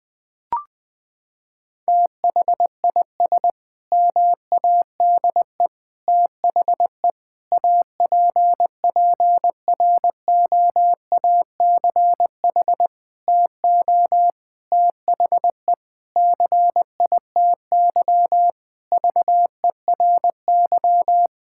Morse code, heard as a keyed tone, about 21 wpm, 700 Hz. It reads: THIS MADE THE APPROACH TO THE CITY VERY
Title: Sentences from Top 1000 Words - Review Part 6 - 20wpm